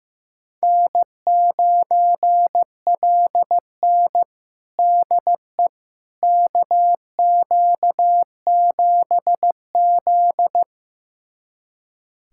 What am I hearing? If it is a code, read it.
Morse code, heard as a keyed tone, about 15 wpm, 700 Hz. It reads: N9LN DE KQ7Z
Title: USA Callsign DE Callsign - 4 Characters - 15wpm